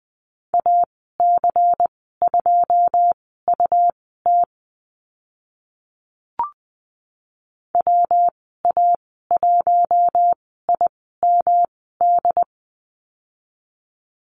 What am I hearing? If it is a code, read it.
Morse code, heard as a keyed tone, about 20 wpm, spaced as for 14 wpm, 700 Hz. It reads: AC2UT WA1IMD
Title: 128 Lesson 21g - Number 6 - Call Signs 20wpm